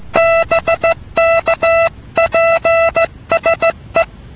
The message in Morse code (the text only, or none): BK PSE